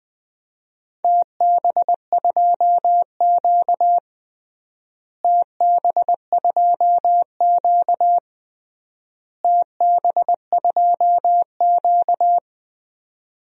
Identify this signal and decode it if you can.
Morse code, heard as a keyed tone, about 20 wpm, 700 Hz. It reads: TB2Q TB2Q TB2Q
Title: DXCC-100 Call Signs - 4 Characters - Repeated 3x - 20wpm